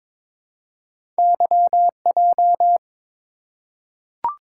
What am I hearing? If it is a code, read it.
Morse code, heard as a keyed tone, about 22 wpm, 700 Hz. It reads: YJ
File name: practice / 22wpm / Two Letter-Number Combinations - 22wpm